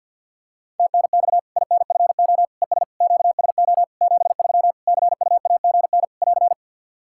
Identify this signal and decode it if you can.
Morse code, heard as a keyed tone, about 52 wpm, 700 Hz. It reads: TNX INFO ES QSO 73 CUAGN <AR>
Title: K1USN SST QSO Practice 50wpm